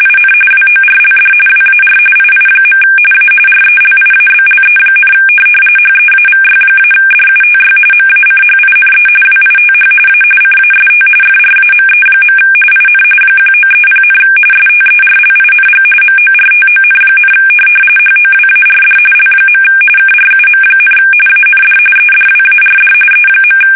RTTY, about 45.45 baud. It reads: IN WORLD WAR II. A RELIABLE, HEAVY-DUTY MACHINE WITH A CAST FRAME. IN 1930, STERLING MORTON, HOWARD L. KRUM, AND EDWARD E.
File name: fsk850 / teletype-corp